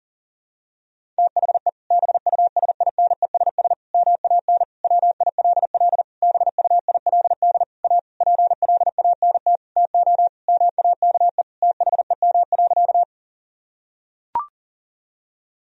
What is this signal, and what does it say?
Morse code, heard as a keyed tone, about 37 wpm, 700 Hz. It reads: THE BUSINESS MAN WILL BUILD A PLANT TO MAKE THEM.
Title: Sentences from Top 1000 Words - Review Part 1 - 40wpm